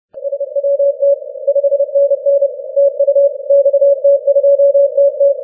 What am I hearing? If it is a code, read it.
Morse code, heard as a keyed tone, about 30 wpm, 560 Hz. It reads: SWT 5NN TU XT2TT
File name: wp / xt2tt 15m cw 02 mar 2013 1716z